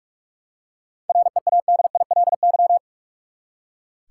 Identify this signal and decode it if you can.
Morse code, heard as a keyed tone, about 42 wpm, 700 Hz. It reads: READILY
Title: Seven Letter Words - 45wpm